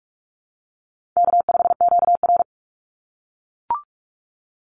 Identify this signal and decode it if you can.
Morse code, heard as a keyed tone, about 44 wpm, 700 Hz. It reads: K5QR
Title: NA SOTA Chaser Callsigns 45wpm